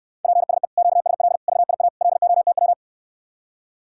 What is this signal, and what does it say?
Morse code, heard as a keyed tone, about 69 wpm, 700 Hz. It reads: PSE QSL VIA LOTW